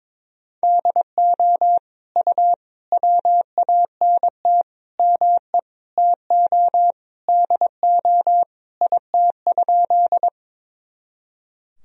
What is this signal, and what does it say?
Morse code, heard as a keyed tone, about 22 wpm, 700 Hz. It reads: DO U WANT ME TO DO IT?
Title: Sentences from Top 200 Words and Abbreviations 22wpm